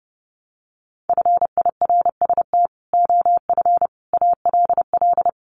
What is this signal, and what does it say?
Morse code, heard as a keyed tone, about 28 wpm, 700 Hz. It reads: FIRST OF ALL
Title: Sets of 3 Words 30wpm